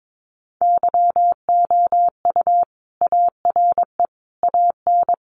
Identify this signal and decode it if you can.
Morse code, heard as a keyed tone, about 22 wpm, 700 Hz. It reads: YOU ARE AN
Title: Sentences from Top 1400 Words 22wpm